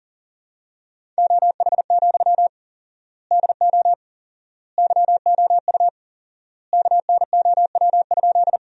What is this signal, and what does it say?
Morse code, heard as a keyed tone, about 40 wpm, 700 Hz. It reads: OH, DO YOU KNOW?
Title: 4x Word-Spacing - Sentences from Top 500 Words - 40wpm